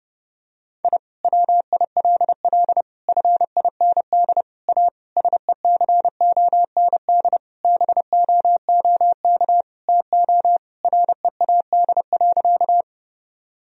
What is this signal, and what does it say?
Morse code, heard as a keyed tone, about 30 wpm, 700 Hz. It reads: I WILL FIND A SECOND BOOK TO READ.
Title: Sentences from Top 1000 Words - Review Part 1 - 30wpm